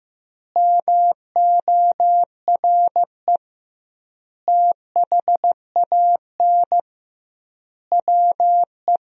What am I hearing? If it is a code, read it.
Morse code, heard as a keyed tone, about 15 wpm, 700 Hz. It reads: MORE THAN WE
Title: 2x Word-Spacing - Sentences from Top 100 Words - 15wpm